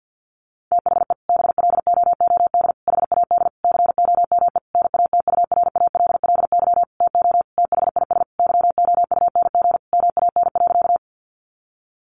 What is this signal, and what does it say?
Morse code, heard as a keyed tone, about 50 wpm, 700 Hz. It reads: THE BLOOD HAD COME NATURALLY TO THIS YOUNG MAN.
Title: Sentences from Top 1300 Words 50wpm